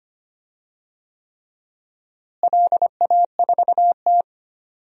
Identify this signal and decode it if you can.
Morse code, heard as a keyed tone, about 25 wpm, 700 Hz. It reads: LA4T
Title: DXCC-100 Call Signs - 4 Characters - Repeated 3x - 25wpm